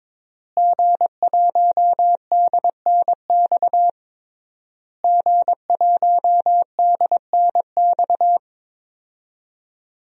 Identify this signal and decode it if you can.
Morse code, heard as a keyed tone, about 22 wpm, 700 Hz. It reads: G1DNX G1DNX
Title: DXCC-100 Call Signs - 5 Characters - Repeated 3x - 22wpm